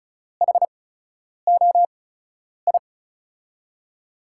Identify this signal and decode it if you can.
Morse code, heard as a keyed tone, about 36 wpm, spaced as for 10 wpm, 700 Hz. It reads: HOI
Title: USA Call Signs - 5 Characters - 8x Character Spacing - 35wpm